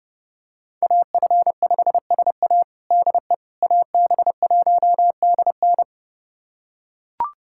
Morse code, heard as a keyed tone, about 30 wpm, 700 Hz. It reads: AF5SA DE AB1DN
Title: USA Callsign DE Callsign - 5 Characters - 30wpm